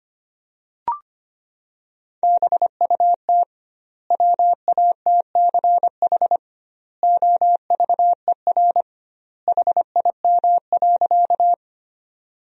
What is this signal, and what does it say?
Morse code, heard as a keyed tone, about 25 wpm, 700 Hz. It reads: BUT WATCH OVER HIM.
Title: 2x Word-Spacing - Sentences from Top 300 Words - 25wpm